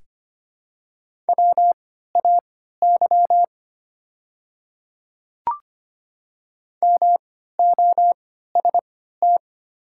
Morse code, heard as a keyed tone, about 25 wpm, spaced as for 14 wpm, 700 Hz. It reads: WAY MOST
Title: 3x Character Spacing - Top 100 Words - 25wpm